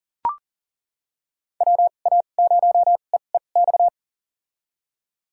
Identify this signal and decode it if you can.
Morse code, heard as a keyed tone, about 40 wpm, spaced as for 29 wpm, 700 Hz. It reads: WA0EEX
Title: USA Call Signs - 6 Characters - 2x Character Spacing - 40wpm